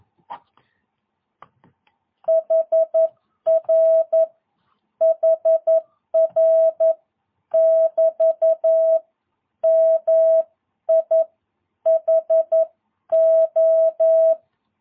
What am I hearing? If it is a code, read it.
Morse code, heard as a keyed tone, about 11 wpm, spaced as for 9 wpm, 650 Hz. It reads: HR HR <BT> MIHO
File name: jh8cbh / 2022cw